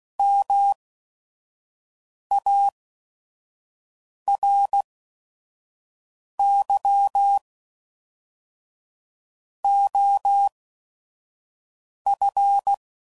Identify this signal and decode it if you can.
Morse code, heard as a keyed tone, about 16 wpm, spaced as for 5 wpm, 800 Hz. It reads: MARY OF
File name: CW Code Groups @5WPM